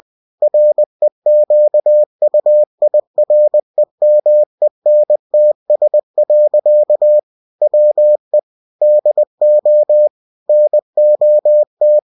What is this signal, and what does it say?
Morse code, heard as a keyed tone, about 20 wpm, 600 Hz. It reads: REQUIREMENTS. WE DO NOT